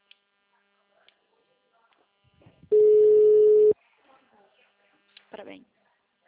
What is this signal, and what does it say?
Ringback tone (Europe, 425 Hz), with speech.